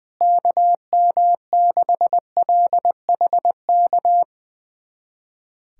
Morse code, heard as a keyed tone, about 20 wpm, 700 Hz. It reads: KM6LHK
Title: USA Callsign DE Callsign - 6 Characters - 20wpm